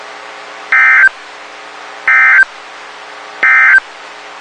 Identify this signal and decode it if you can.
Emergency Alert System (SAME) tones, heard as harsh buzzing bursts. It is heard from 0.6 to 3.9 s.
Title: WXK91 NOAA Weather Radio